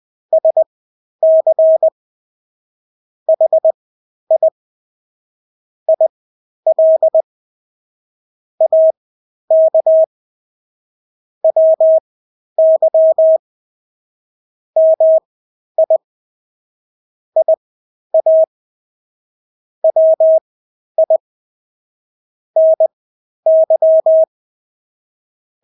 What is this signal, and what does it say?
Morse code, heard as a keyed tone, about 20 wpm, spaced as for 11 wpm, 640 Hz. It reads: SC HI IL AK WY MI IA WI NY
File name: State-Abbreviations-10